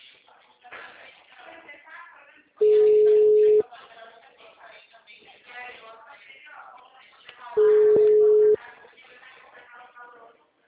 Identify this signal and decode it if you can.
Ringback tone (Europe, 425 Hz), with speech.